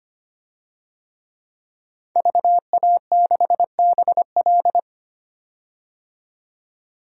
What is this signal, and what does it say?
Morse code, heard as a keyed tone, about 25 wpm, 690 Hz. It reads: VA6BL
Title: DXCC-100 Call Signs - 5 Characters - 25wpm